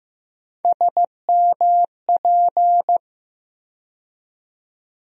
Morse code, heard as a keyed tone, about 15 wpm, 700 Hz. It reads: SMP
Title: USA Call Sign Postfixes - 3 Letters 15wpm